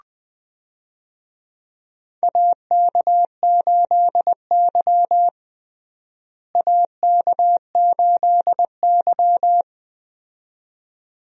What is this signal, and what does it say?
Morse code, heard as a keyed tone, about 20 wpm, 700 Hz. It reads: AK8Y AK8Y
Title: USA Call Signs - 4 Characters - Repeated 2x - 20wpm